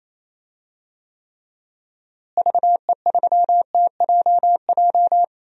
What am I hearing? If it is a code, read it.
Morse code, heard as a keyed tone, about 28 wpm, 700 Hz. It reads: VE3TJJ
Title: DXCC-100 Call Signs - 6 Characters - Repeated 3x - 28wpm